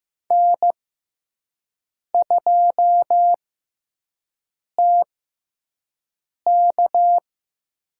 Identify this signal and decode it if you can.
Morse code, heard as a keyed tone, about 15 wpm, spaced as for 5 wpm, 700 Hz. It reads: N2TK
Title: USA Call Signs - 5 Characters - 6x Character Spacing - 15wpm